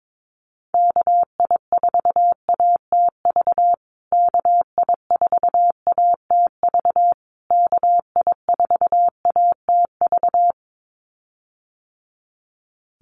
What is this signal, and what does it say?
Morse code, heard as a keyed tone, about 22 wpm, 700 Hz. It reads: KI4ATV KI4ATV KI4ATV
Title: USA Call Signs Repeated 3x 22wpm